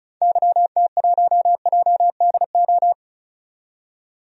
Morse code, heard as a keyed tone, about 36 wpm, 690 Hz. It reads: YT1JDO
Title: DXCC-100 Call Signs - 6 Characters - 35wpm